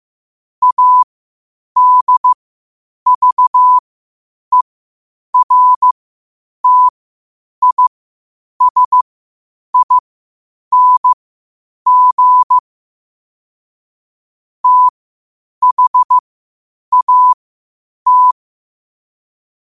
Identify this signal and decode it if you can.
Morse code, heard as a keyed tone, about 15 wpm, spaced as for 8 wpm, 1000 Hz. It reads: ADVERTISING THAT
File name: qotdfiles / 8wpm